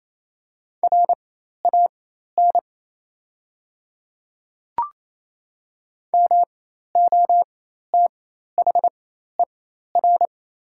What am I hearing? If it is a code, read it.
Morse code, heard as a keyed tone, about 28 wpm, spaced as for 13 wpm, 700 Hz. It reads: RAN MOTHER
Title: 4x Character Spacing - Top 500 Words - 28wpm